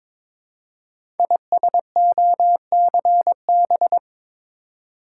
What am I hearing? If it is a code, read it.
Morse code, heard as a keyed tone, about 22 wpm, 700 Hz. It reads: ISOCB
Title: Code Groups - 5 Letters - 22wpm